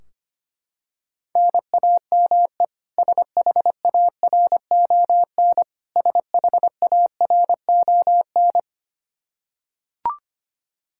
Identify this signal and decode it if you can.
Morse code, heard as a keyed tone, about 25 wpm, 700 Hz. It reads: NAME SHARON SHARON
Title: K1USN SST QSO Practice 25wpm